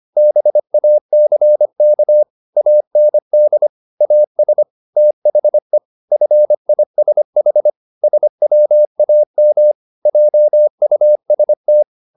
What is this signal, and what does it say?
Morse code, heard as a keyed tone, about 25 wpm, 600 Hz. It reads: BACK AND AS THE FISH SWAM JUST